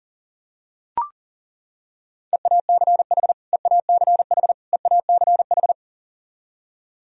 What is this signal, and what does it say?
Morse code, heard as a keyed tone, about 37 wpm, 700 Hz. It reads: EACH EACH EACH
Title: Top 100 Words Repeated 40wpm